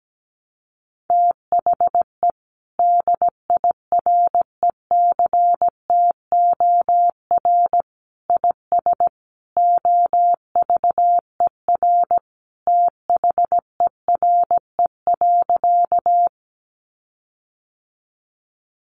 Morse code, heard as a keyed tone, about 17 wpm, 700 Hz. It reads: THE DIRECTOR IS OVER THERE.
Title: Sentences from Top 1300 Words 17wpm